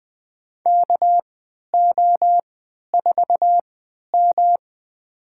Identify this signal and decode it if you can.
Morse code, heard as a keyed tone, about 20 wpm, spaced as for 11 wpm, 700 Hz. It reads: KO4M
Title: USA Call Signs - 4 Characters - 3x Character Spacing - 20wpm